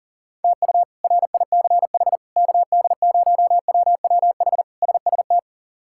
Morse code, heard as a keyed tone, about 40 wpm, 700 Hz. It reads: TU RICH KD0WWH SST
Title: K1USN SST QSO Practice - 40wpm